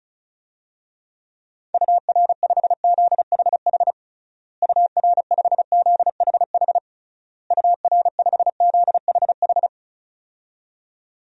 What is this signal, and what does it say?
Morse code, heard as a keyed tone, about 36 wpm, 700 Hz. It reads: UR5ZHH UR5ZHH UR5ZHH
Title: DXCC-100 Call Signs - 6 Characters - Repeated 3x - 35wpm